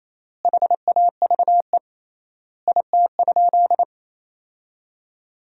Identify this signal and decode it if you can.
Morse code, heard as a keyed tone, about 28 wpm, 700 Hz. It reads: HAVE IT?
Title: 3x Word-Spacing - Sentences from Top 100 Words - 28wpm